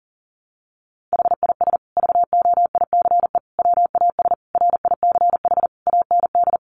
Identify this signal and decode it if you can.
Morse code, heard as a keyed tone, about 40 wpm, 700 Hz. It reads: HIS VOICE WAS RICH AND
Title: Sentences from Top 1100 Words 40wpm